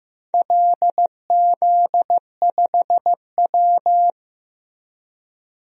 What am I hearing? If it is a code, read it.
Morse code, heard as a keyed tone, about 15 wpm, 700 Hz. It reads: LZ5W
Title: DXCC-100 Call Signs - 4 Characters - Repeated 2x - 15wpm